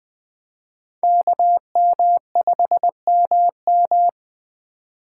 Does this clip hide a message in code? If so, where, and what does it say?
Yes: Morse code from 1.0 to 4.1 s, heard as a keyed tone, about 19 wpm, 700 Hz. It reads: KM5MM